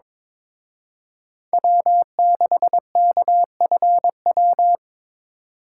Morse code, heard as a keyed tone, about 26 wpm, 700 Hz. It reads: W6KFW
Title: Speed Racing - USA 1x3 Call Signs 22wpm